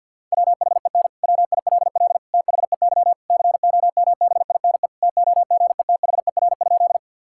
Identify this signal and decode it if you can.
Morse code, heard as a keyed tone, about 50 wpm, 700 Hz. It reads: WHEN WILL THEY COMBINE TOGETHER?
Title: Sentences from Top 1600 Words 50wpm